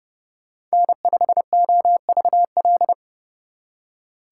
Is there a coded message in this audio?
Yes: Morse code from 0.7 to 2.9 s, heard as a keyed tone, about 39 wpm, 700 Hz. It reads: N5OVL